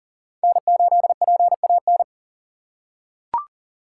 Morse code, heard as a keyed tone, about 40 wpm, 700 Hz. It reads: N8PAN
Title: USA Call Signs - 5 Characters - 2x Character Spacing - 40wpm